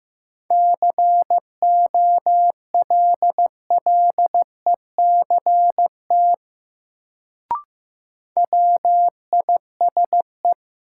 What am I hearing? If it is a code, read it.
Morse code, heard as a keyed tone, about 15 wpm, 700 Hz. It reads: COLLECT WISE
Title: Sentences from Top 1600 Words - New Word Drill - 15wpm